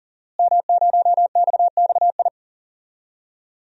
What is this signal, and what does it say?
Morse code, heard as a keyed tone, about 40 wpm, 700 Hz. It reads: M0XXI
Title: DXCC-100 Call Signs - 5 Characters - Repeated 3x - 40wpm